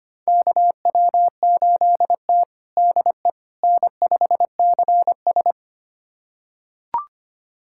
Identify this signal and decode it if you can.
Morse code, heard as a keyed tone, about 25 wpm, 700 Hz. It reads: KW8T DE N5CS
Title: USA Callsign DE Callsign - 4 Characters - 25wpm